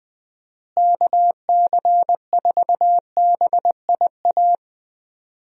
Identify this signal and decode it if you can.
Morse code, heard as a keyed tone, about 20 wpm, 700 Hz. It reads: KC4BIA
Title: USA Call Signs - 6 Characters - Repeated 2x - 20wpm